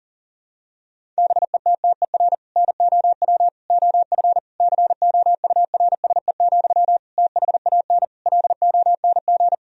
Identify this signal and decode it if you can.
Morse code, heard as a keyed tone, about 40 wpm, 700 Hz. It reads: BETTER NOW OF COURSE, THAN LONG
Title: Sentences from Top 1000 Words - Review Part 3 - 40wpm